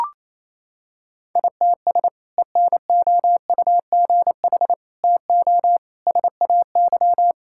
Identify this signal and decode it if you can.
Morse code, heard as a keyed tone, about 28 wpm, 700 Hz. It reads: ITS ENOUGH TO SAY
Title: Sentences from Top 1000 Words - Review Part 6 - 28wpm